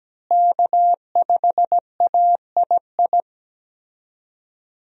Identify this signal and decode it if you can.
Morse code, heard as a keyed tone, about 17 wpm, 700 Hz. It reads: K5AII
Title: USA Call Signs - 5 Characters - Repeated 2x - 17wpm